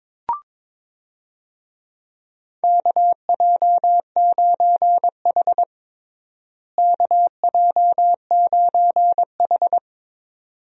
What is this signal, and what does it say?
Morse code, heard as a keyed tone, about 21 wpm, 700 Hz. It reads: KJ9H KJ9H
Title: USA Call Signs - 4 Characters - Repeated 2x - 22wpm